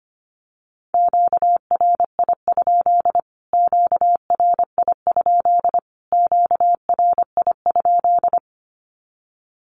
Morse code, heard as a keyed tone, about 25 wpm, 700 Hz. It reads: QRI? QRI? QRI?
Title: Q-Codes Repeated 3x 25wpm